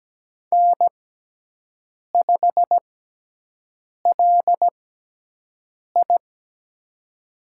Morse code, heard as a keyed tone, about 17 wpm, spaced as for 6 wpm, 700 Hz. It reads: N5LI
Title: USA Call Signs - 4 Characters - 6x Character Spacing - 17wpm